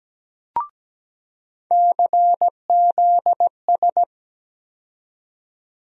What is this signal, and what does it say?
Morse code, heard as a keyed tone, about 17 wpm, 700 Hz. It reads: CZS
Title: USA Call Sign Postfixes - 3 Letters 17wpm